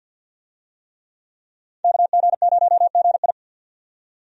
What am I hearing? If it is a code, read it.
Morse code, heard as a keyed tone, about 50 wpm, 700 Hz. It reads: KG0GI
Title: USA Call Signs - 5 Characters - 50wpm